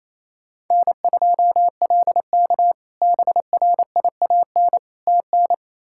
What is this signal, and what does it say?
Morse code, heard as a keyed tone, about 28 wpm, 700 Hz. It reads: N2LK BRIAN TN
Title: K1USN SST QSO Practice - 28wpm